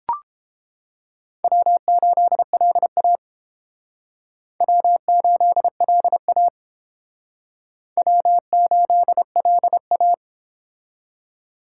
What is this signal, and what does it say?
Morse code, heard as a keyed tone, about 31 wpm, 700 Hz. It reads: W8LA W8LA W8LA